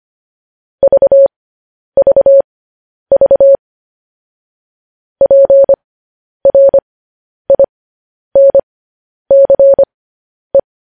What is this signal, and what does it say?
Morse code, heard as a keyed tone, about 25 wpm, spaced as for 10 wpm, 550 Hz. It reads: VVV PRINCE
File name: podcast / News headlines-1x-25x10